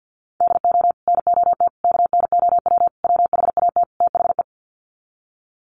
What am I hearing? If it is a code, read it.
Morse code, heard as a keyed tone, about 50 wpm, 700 Hz. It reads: DO NOT KNOW WHAT THE
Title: Sets of 5 Words 50wpm